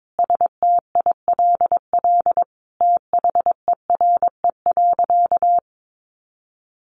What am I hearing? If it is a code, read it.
Morse code, heard as a keyed tone, about 22 wpm, 700 Hz. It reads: STILL THERE.
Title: Sentences from Top 1500 Words 22wpm